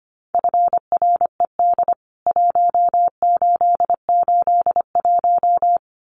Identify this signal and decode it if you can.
Morse code, heard as a keyed tone, about 25 wpm, 700 Hz. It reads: FRED 1881
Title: CWT Exchange 25wpm